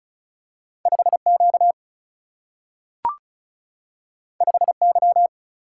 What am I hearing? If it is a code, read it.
Morse code, heard as a keyed tone, about 36 wpm, 700 Hz. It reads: 5 Q 5 Y
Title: Single-Letter Single-Number 35wpm